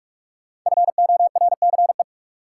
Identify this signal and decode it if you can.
Morse code, heard as a keyed tone, about 48 wpm, 700 Hz. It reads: FORCE
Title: Five Letter Words - 45wpm